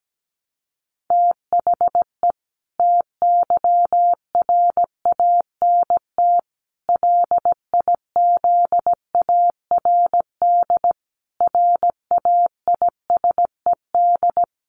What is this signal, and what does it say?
Morse code, heard as a keyed tone, about 17 wpm, 700 Hz. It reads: THE TYRANT LIZARD RAISED
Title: A Sound Of Thunder 17wpm